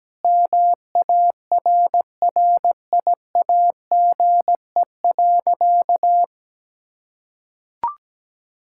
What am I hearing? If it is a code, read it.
Morse code, heard as a keyed tone, about 17 wpm, 700 Hz. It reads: MARRIAGE.
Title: Sentences from Top 1000 Words - Review Part 5 - 17wpm